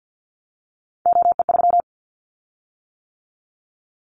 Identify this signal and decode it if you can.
Morse code, heard as a keyed tone, about 50 wpm, 690 Hz. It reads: OE3
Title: Callsign Prefixes for Top 100 DXCC - 50wpm